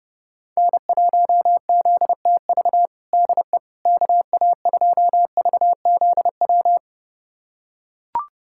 Morse code, heard as a keyed tone, about 30 wpm, 700 Hz. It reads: N1ZTV DE KA2VZW
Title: USA Callsign DE Callsign 30wpm